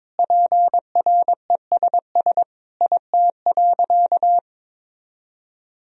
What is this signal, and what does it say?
Morse code, heard as a keyed tone, about 22 wpm, 700 Hz. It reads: PRESS IT.
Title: Sentences from Top 1000 Words - Review Part 1 - 22wpm